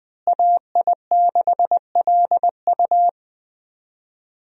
Morse code, heard as a keyed tone, about 20 wpm, 700 Hz. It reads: AI6LU
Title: USA Call Signs - 5 Characters - Repeated 3x - 20wpm